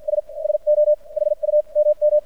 Morse code, heard as a keyed tone, about 52 wpm, spaced as for 43 wpm, 600 Hz. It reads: S SO HU MM
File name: archived / CRSnet